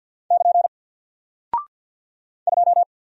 Morse code, heard as a keyed tone, about 50 wpm, 700 Hz. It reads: <KN> 2
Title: Single Character (Letter:Number:Punctuation:Prosign) 50wpm